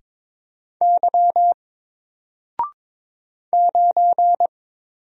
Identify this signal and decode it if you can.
Morse code, heard as a keyed tone, about 22 wpm, 700 Hz. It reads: Y 9
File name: Single-Letter-Number-22wpm